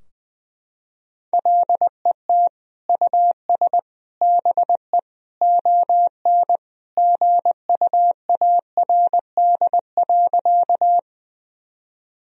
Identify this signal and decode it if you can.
Morse code, heard as a keyed tone, about 20 wpm, 700 Hz. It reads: LET US BE ON GUARD.
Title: Sentences from Top 1000 Words - Review Part 4 - 20wpm